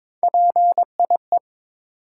Morse code, heard as a keyed tone, about 23 wpm, 700 Hz. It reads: PIE
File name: practice / 22wpm / Difficult Words with Hints - 22wpm